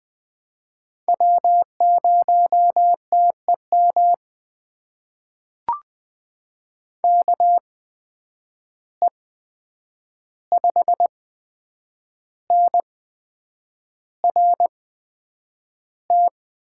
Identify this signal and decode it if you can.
Morse code, heard as a keyed tone, about 20 wpm, spaced as for 8 wpm, 700 Hz. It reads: W0TEM KE5NRT
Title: USA Call Signs - 8x Character Spacing - 20wpm